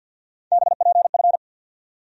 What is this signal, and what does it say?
Morse code, heard as a keyed tone, about 47 wpm, 700 Hz. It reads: BPF
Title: USA Call Sign Postfixes - 3 Letters 50wpm